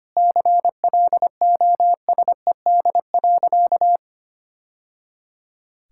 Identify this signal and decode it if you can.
Morse code, heard as a keyed tone, about 25 wpm, 700 Hz. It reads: CLOSED.
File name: Easy Sentences with Hints - 25wpm